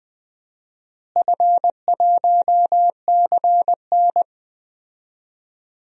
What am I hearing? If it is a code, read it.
Morse code, heard as a keyed tone, about 21 wpm, 700 Hz. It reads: F1CN
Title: DXCC-100 Call Signs - 4 Characters - Repeated 3x - 20wpm